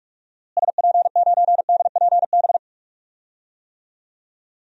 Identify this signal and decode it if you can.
Morse code, heard as a keyed tone, about 41 wpm, 700 Hz. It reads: SP9DPB
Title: DXCC-100 Call Signs - 6 Characters - Repeated 3x - 45wpm